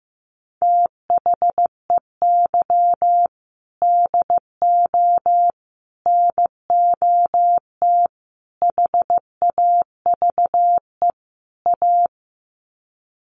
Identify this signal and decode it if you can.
Morse code, heard as a keyed tone, about 15 wpm, 700 Hz. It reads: THEY DO NOT HAVE A
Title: Sets of 5 Words 15wpm